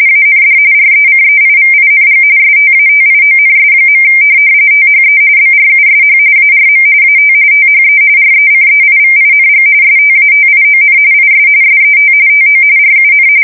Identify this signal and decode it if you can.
RTTY, about 45.45 baud. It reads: AND RECEIVING STATIONS. THE BRUSH ARM AT THE RECEIVING STATION IS SET TO RUN
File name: fsk170 / printing-telegraph